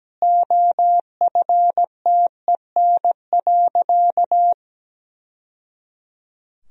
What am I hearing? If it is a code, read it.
Morse code, heard as a keyed tone, about 17 wpm, 700 Hz. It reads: OFTEN.
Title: Sentences from Top 1000 Words - Review Part 2 - 17wpm